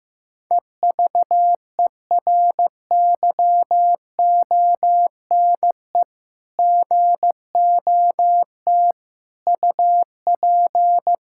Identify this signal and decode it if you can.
Morse code, heard as a keyed tone, about 15 wpm, 700 Hz. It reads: EVERYONE GOT UP
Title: Sentences from Top 1000 Words - Review Part 5 - 15wpm